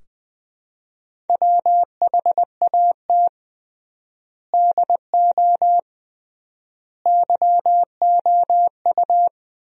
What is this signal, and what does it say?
Morse code, heard as a keyed tone, about 20 wpm, 700 Hz. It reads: WHAT DO YOU
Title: 3x Word-Spacing - Sentences from Top 100 Words - 20wpm